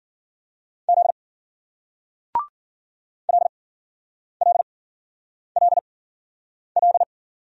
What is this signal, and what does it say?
Morse code, heard as a keyed tone, about 47 wpm, spaced as for 9 wpm, 700 Hz. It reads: Z LLLL
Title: Speed-Racing Single Letter-Number 40wpm